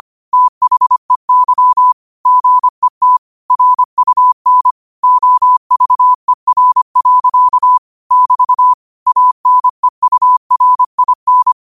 Morse code, heard as a keyed tone, about 25 wpm, 1000 Hz. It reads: THEY GET RUN OVER. <BT> ANEURIN